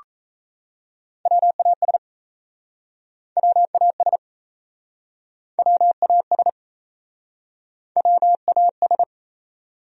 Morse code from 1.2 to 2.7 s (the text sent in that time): WAS